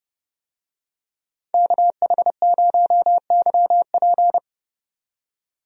Morse code, heard as a keyed tone, about 31 wpm, 700 Hz. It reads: KH0YP
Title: USA Call Signs - 5 Characters - 30wpm